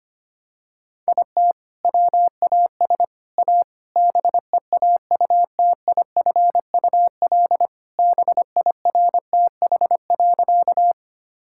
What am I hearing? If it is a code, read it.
Morse code, heard as a keyed tone, about 25 wpm, 700 Hz. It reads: IT WAS A BEAUTIFUL BIRTH.
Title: Sentences from Top 1000 Words - Review Part 5 - 25wpm